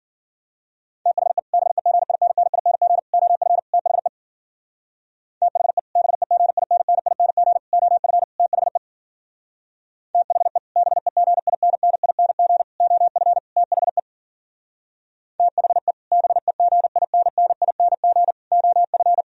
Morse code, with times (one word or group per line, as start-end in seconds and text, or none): 1.1-1.4 THE
1.5-3.0 BEGINNING
3.1-3.6 OF
3.7-4.1 THE
5.4-5.8 THE
6.0-7.6 BEGINNING
7.7-8.2 OF
8.4-8.8 THE
10.1-10.6 THE
10.8-12.6 BEGINNING
12.8-13.4 OF
13.6-14.0 THE
15.4-15.9 THE
16.1-18.3 BEGINNING
18.5-19.2 OF